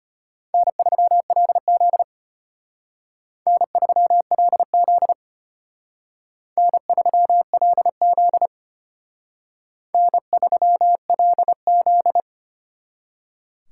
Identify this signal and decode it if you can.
Morse code, heard as a keyed tone, about 32 wpm, 700 Hz. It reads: N3LZ N3LZ N3LZ N3LZ